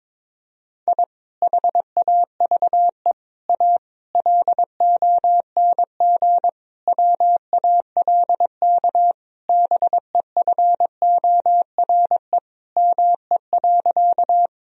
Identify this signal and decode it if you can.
Morse code, heard as a keyed tone, about 22 wpm, 700 Hz. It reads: I HAVE A LONG WALK BEFORE ME.
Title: Sentences from Top 1000 Words - Review Part 1 - 22wpm